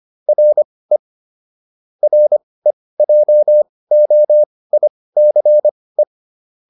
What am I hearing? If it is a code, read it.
Morse code, heard as a keyed tone, about 25 wpm, spaced as for 18 wpm, 600 Hz. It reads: RE REJOICE